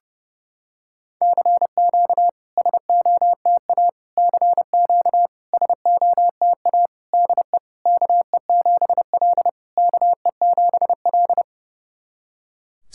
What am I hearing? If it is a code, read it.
Morse code, heard as a keyed tone, about 32 wpm, 690 Hz. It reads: CQ SOTA CQ SOTA DE KE7L KE7L
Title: SOTA QSO Practice 30wpm